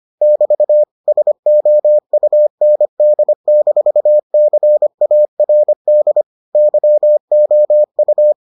Morse code, heard as a keyed tone, about 25 wpm, 600 Hz. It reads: <BT> SOUND-CARD YOU